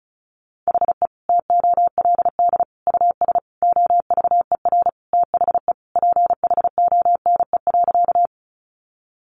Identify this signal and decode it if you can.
Morse code, heard as a keyed tone, about 34 wpm, 700 Hz. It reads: HE TOLD US OVER THE PHONE.
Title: Sentences from Top 1100 Words 35wpm